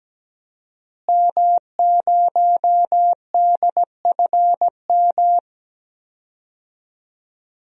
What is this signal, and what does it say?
Morse code, heard as a keyed tone, about 17 wpm, 700 Hz. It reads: M0DFM
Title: DXCC-100 Call Signs - 5 Characters - Repeated 3x - 17wpm